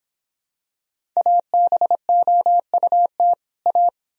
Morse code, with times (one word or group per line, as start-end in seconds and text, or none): 1.2-3.3 ABOUT
3.7-3.9 A